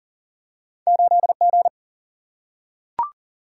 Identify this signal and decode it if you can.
Morse code, heard as a keyed tone, about 40 wpm, 700 Hz. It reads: 8 G
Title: Single-Letter Single-Number 40wpm